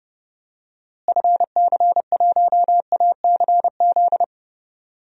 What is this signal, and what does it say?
Morse code, heard as a keyed tone, about 31 wpm, 700 Hz. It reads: FC1ACZ
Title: DXCC-100 Call Signs - 6 Characters - Repeated 2x - 30wpm